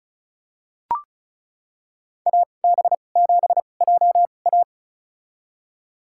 Morse code, heard as a keyed tone, about 36 wpm, spaced as for 25 wpm, 700 Hz. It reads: AB7JA
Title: USA Call Signs - 5 Characters - 2x Character Spacing - 35wpm